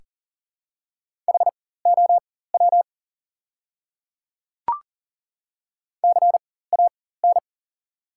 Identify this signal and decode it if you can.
Morse code, heard as a keyed tone, about 40 wpm, spaced as for 19 wpm, 690 Hz. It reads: HOW CAN
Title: 4x Character Spacing - Top 200 Words - 40wpm